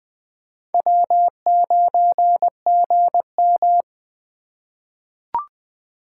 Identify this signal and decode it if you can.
Morse code, heard as a keyed tone, about 20 wpm, 700 Hz. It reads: W9GM
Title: 2023 POTA Hunter Call Signs - 20wpm